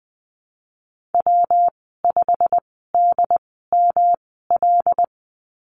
Morse code, heard as a keyed tone, about 20 wpm, spaced as for 14 wpm, 700 Hz. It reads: W5DML
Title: 179 Lesson 29g - Slash - Call Signs 20wpm